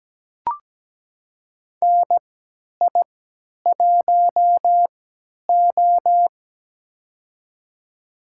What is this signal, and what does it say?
Morse code, heard as a keyed tone, about 17 wpm, spaced as for 10 wpm, 700 Hz. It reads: NI1O
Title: USA Call Signs - 4 Characters - 3x Character Spacing - 17wpm